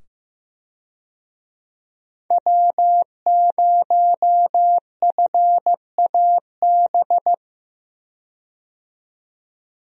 Morse code, heard as a keyed tone, about 15 wpm, 700 Hz. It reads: W0FAB
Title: USA Call Signs - 5 Characters - 15wpm